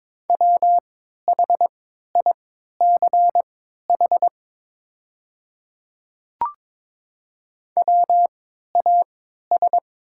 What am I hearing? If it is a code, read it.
Morse code, heard as a keyed tone, about 23 wpm, spaced as for 13 wpm, 700 Hz. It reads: WHICH WAS
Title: 3x Character Spacing - Top 100 Words - 22wpm